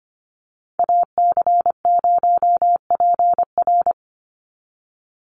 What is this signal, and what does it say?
Morse code, heard as a keyed tone, about 25 wpm, 700 Hz. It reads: AC0PR
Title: NA SOTA Chaser Callsigns 25wpm